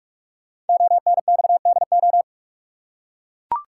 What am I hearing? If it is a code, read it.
Morse code, heard as a keyed tone, about 44 wpm, 700 Hz. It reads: ONXDO
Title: Code Groups - 5 Letters - 45wpm